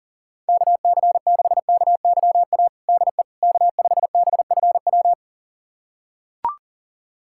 Morse code, heard as a keyed tone, about 37 wpm, 700 Hz. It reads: KC6KYA DE K5BFW
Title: USA Callsign DE Callsign 40wpm